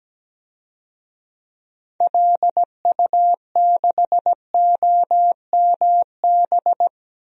Morse code, heard as a keyed tone, about 17 wpm, 700 Hz. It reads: LU6OMB
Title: DXCC-100 Call Signs - 6 Characters - Repeated 3x - 17wpm